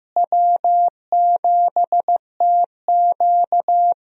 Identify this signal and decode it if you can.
Morse code, heard as a keyed tone, about 16 wpm, 700 Hz. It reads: W7TQ
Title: USA Call Signs - 4 Characters - Repeated 3x - 15wpm